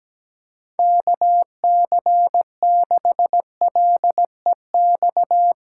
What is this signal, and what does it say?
Morse code, heard as a keyed tone, about 17 wpm, 700 Hz. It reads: KC6LEX
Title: USA Callsign DE Callsign 17wpm